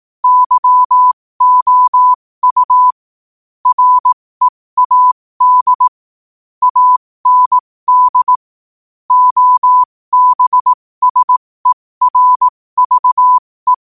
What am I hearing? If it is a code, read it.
Morse code, heard as a keyed tone, about 18 wpm, spaced as for 15 wpm, 1000 Hz. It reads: YOU READ AND OBSERVE
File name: qotdfiles / 15wpm